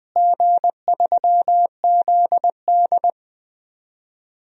Morse code, heard as a keyed tone, about 20 wpm, 700 Hz. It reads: G3ZD
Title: DXCC-100 Call Signs - 4 Characters - Repeated 3x - 20wpm